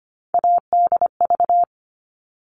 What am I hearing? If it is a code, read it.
Morse code, heard as a keyed tone, about 25 wpm, 700 Hz. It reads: ADV
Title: All Cut Numbers - 3-Digit 25wpm